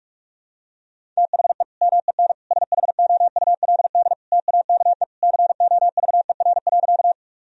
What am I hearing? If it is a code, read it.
Morse code, heard as a keyed tone, about 41 wpm, 700 Hz. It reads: THE MEN SHOULD TAKE COVER.
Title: Sentences from Top 1000 Words - Review Part 1 - 45wpm